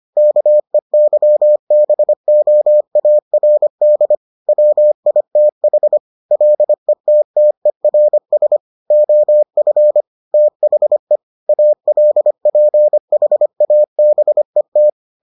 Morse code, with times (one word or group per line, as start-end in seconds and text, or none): 0.2-4.1 KEYBOARD
4.5-6.0 WITH
6.3-8.6 LETTERS
8.9-10.0 OF
10.3-11.2 THE
11.5-14.9 ALPHABET